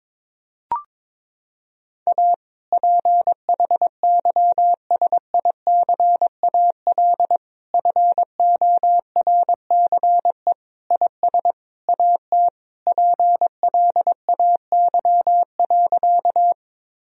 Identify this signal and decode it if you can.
Morse code, heard as a keyed tone, about 22 wpm, 700 Hz. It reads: A PHYSICAL FORCE IS AT PLAY.
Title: Sentences from Top 1000 Words - Review Part 4 - 22wpm